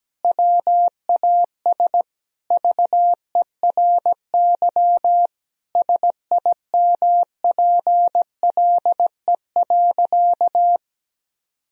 Morse code, heard as a keyed tone, about 17 wpm, 700 Hz. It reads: WAS VERY SIMPLE.
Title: Sentences from Top 1000 Words - Review Part 4 - 17wpm